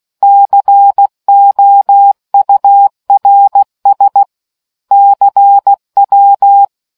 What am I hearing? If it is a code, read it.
Morse code, heard as a keyed tone, about 16 wpm, 800 Hz. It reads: COURS CW